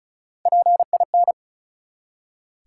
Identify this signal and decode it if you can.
Morse code, heard as a keyed tone, about 34 wpm, 700 Hz. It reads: PIN
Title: Three Letter Words - 35wpm